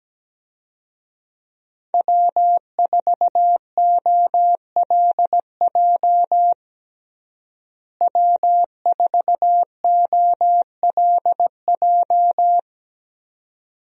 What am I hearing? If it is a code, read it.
Morse code, heard as a keyed tone, about 17 wpm, 700 Hz. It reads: W4OLJ W4OLJ
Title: USA Call Signs - 5 Characters - Repeated 2x - 17wpm